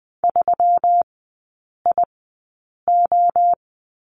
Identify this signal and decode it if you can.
Morse code, heard as a keyed tone, about 20 wpm, spaced as for 8 wpm, 690 Hz. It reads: 3IO
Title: 145 Lesson 24e - Letter y - Three Characters 20wpm